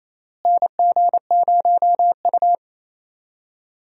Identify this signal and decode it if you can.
Morse code, heard as a keyed tone, about 28 wpm, 700 Hz. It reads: NG0U
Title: USA Call Signs - 4 Characters - Repeated 3x - 28wpm